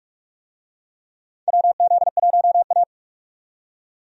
Morse code, heard as a keyed tone, about 42 wpm, 700 Hz. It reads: WZ1A
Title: USA Call Signs - 4 Characters - 45wpm